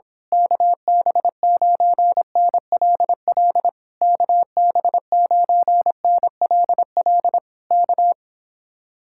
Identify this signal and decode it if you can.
Morse code, heard as a keyed tone, about 26 wpm, 700 Hz. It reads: KB9NLL KB9NLL K
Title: Speed-Racing Common QSO Phrases Encore - 22wpm